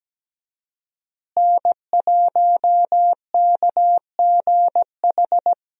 Morse code, heard as a keyed tone, about 17 wpm, 700 Hz. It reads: N1KGH
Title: USA Call Signs - 5 Characters - Repeated 2x - 17wpm